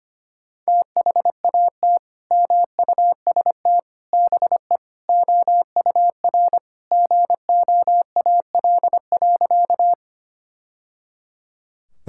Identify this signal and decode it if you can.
Morse code, heard as a keyed tone, about 25 wpm, 700 Hz. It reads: THAT MUST BE OUR GOAL.
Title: Sentences from Top 1000 Words - Review Part 5 - 25wpm